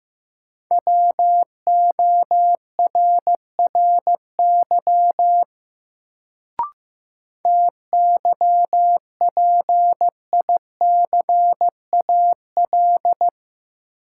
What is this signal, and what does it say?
Morse code, heard as a keyed tone, about 15 wpm, 700 Hz. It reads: WORRY TYPICAL
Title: Sentences from Top 1100 Words - New Word Drill - 15wpm